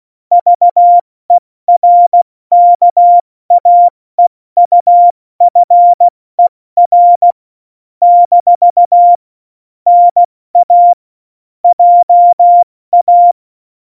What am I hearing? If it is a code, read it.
Morse code, heard as a keyed tone, about 16 wpm, spaced as for 14 wpm, 700 Hz. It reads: VERKAEUFER - NA JA
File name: Morsekurs / Witze / Witz 15. Januar 2019, 70 F80